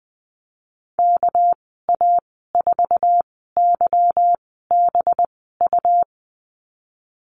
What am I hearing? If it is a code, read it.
Morse code, heard as a keyed tone, about 20 wpm, spaced as for 15 wpm, 700 Hz. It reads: KA4YBU
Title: 207 Lesson 33g - Letter j - Call Signs 20wpm